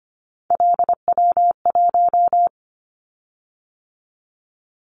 Morse code, heard as a keyed tone, about 25 wpm, 700 Hz. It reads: LW1
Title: Callsign Prefixes for Top 50 DXCC - 25wpm